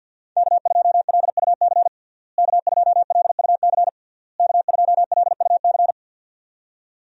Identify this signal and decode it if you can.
Morse code, heard as a keyed tone, about 46 wpm, 700 Hz. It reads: K2LUC K2LUC K2LUC
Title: USA Call Signs - 5 Characters - Repeated 3x - 50wpm